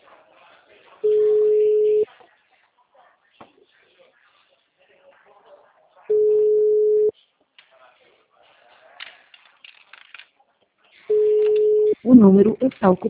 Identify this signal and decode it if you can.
Ringback tone (Europe, 425 Hz), with speech.